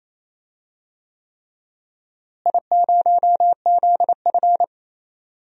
Morse code, heard as a keyed tone, about 28 wpm, 700 Hz. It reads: I0ZF
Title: DXCC-100 Call Signs - 4 Characters - Repeated 3x - 28wpm